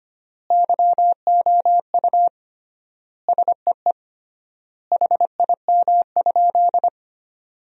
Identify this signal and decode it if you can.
Morse code, heard as a keyed tone, about 25 wpm, 700 Hz. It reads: YOU SEE HIM?
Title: 3x Word-Spacing - Sentences from Top 100 Words - 25wpm